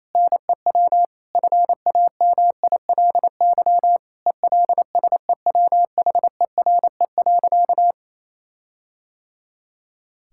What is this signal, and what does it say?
Morse code, heard as a keyed tone, about 28 wpm, 700 Hz. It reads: NEW FAMILY ELSEWHERE.
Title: Sentences from Top 1600 Words 28wpm